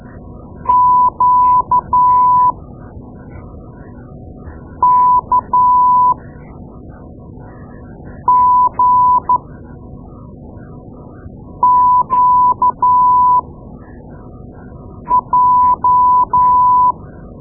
Morse code, heard as a keyed tone, about 12 wpm, spaced as for 4 wpm, 1000 Hz. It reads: QKGQJ
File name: Morse Lessons / Morse Lessons / Tape-2A